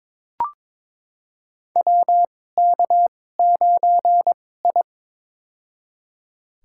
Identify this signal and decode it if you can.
Morse code, heard as a keyed tone, about 22 wpm, spaced as for 16 wpm, 700 Hz. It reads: WK9I
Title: USA Call Signs - 4 Characters - 2x Character Spacing - 22wpm